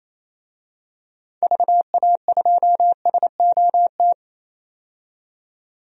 Morse code, heard as a keyed tone, about 28 wpm, 700 Hz. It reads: VA2SOT
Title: DXCC-100 Call Signs - 6 Characters - 28wpm